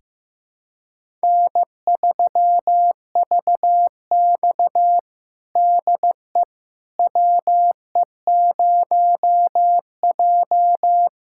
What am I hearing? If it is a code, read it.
Morse code, heard as a keyed tone, about 16 wpm, 700 Hz. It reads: N3VX DE WE0J
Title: USA Callsign DE Callsign - 4 Characters - 15wpm